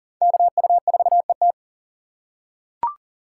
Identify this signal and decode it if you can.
Morse code, heard as a keyed tone, about 40 wpm, 690 Hz. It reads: KU4ET
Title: USA Call Signs - 2x Character Spacing - 40wpm